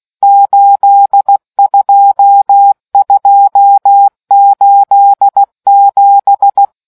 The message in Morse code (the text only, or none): 82287